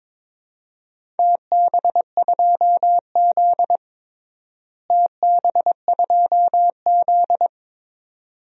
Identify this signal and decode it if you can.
Morse code, heard as a keyed tone, about 22 wpm, 700 Hz. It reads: TB2Z TB2Z
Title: DXCC-100 Call Signs - 4 Characters - Repeated 2x - 22wpm